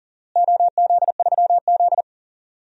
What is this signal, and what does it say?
Morse code, heard as a keyed tone, about 40 wpm, 700 Hz. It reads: OZ3Z
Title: DXCC-100 Call Signs - 4 Characters - Repeated 3x - 40wpm